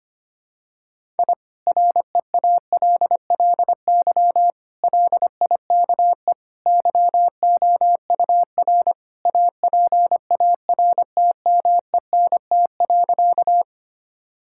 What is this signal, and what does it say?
Morse code, heard as a keyed tone, about 25 wpm, 700 Hz. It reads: I REALLY LIKE YOUR APARTMENT.
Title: Sentences from Top 1000 Words - Review Part 6 - 25wpm